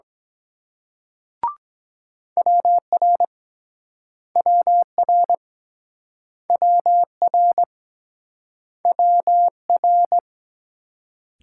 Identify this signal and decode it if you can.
Morse code, heard as a keyed tone, about 23 wpm, 700 Hz. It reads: WR WR WR WR